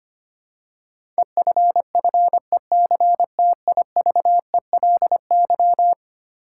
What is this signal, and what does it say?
Morse code, heard as a keyed tone, about 25 wpm, 700 Hz. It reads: EFFECTIVELY
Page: morse-code-ninja-practice-25wpm